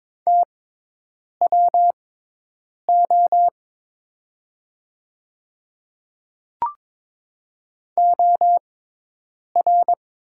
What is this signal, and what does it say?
Morse code, heard as a keyed tone, about 22 wpm, spaced as for 8 wpm, 700 Hz. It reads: TWO OR
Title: 6x Character Spacing - Top 100 Words - 22wpm